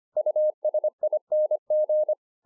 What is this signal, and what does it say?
Morse code, heard as a keyed tone, about 24 wpm, 610 Hz. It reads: USING